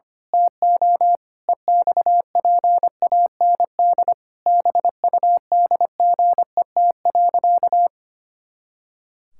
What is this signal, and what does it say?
Morse code, heard as a keyed tone, about 25 wpm, 700 Hz. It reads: TO EXPAND BUDGET.
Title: Pangram Challenge v2 - 25wpm